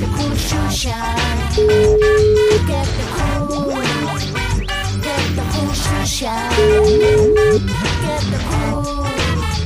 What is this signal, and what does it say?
Ringback tone (Europe, 425 Hz), with noise.